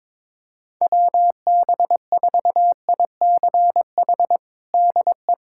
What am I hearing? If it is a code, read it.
Morse code, heard as a keyed tone, about 22 wpm, 700 Hz. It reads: WB4ICH DE
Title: USA Callsign DE Callsign - 6 Characters - 22wpm